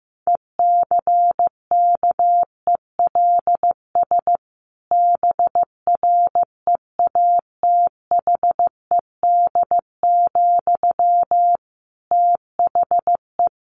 Morse code, heard as a keyed tone, about 15 wpm, 700 Hz. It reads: ECKELS BREATHED, THE
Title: A Sound Of Thunder 15wpm